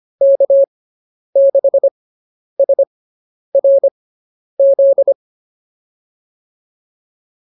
Morse code, heard as a keyed tone, about 25 wpm, spaced as for 10 wpm, 550 Hz. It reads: K6SRZ
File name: POTA109 10